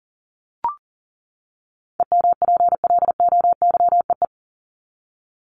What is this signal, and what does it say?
Morse code, heard as a keyed tone, about 40 wpm, 700 Hz. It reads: EMPLOYEE
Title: Top 1501-1750 Words 40wpm